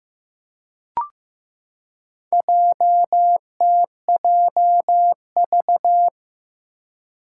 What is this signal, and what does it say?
Morse code, heard as a keyed tone, about 15 wpm, 700 Hz. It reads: JTJV
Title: Four Challenging Letter-Number Combinations 15wpm